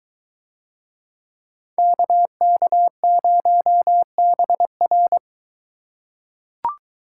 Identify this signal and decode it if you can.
Morse code, heard as a keyed tone, about 21 wpm, 700 Hz. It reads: KK0BR